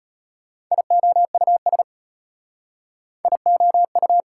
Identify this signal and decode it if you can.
Morse code, heard as a keyed tone, about 33 wpm, 700 Hz. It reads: IOUS IOU